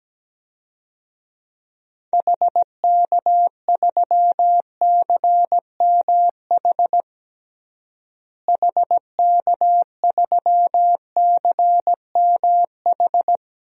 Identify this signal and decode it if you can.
Morse code, heard as a keyed tone, about 17 wpm, 700 Hz. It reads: HK3CMH HK3CMH
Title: DXCC-100 Call Signs - 6 Characters - Repeated 3x - 17wpm